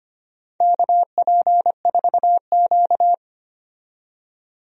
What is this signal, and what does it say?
Morse code, heard as a keyed tone, about 25 wpm, 700 Hz. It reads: KP4Q
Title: DXCC-100 Call Signs - 4 Characters - 25wpm